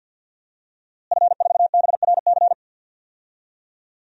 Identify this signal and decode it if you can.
Morse code, heard as a keyed tone, about 49 wpm, 700 Hz. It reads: F4BRC
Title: DXCC-100 Call Signs - 5 Characters - Repeated 2x - 50wpm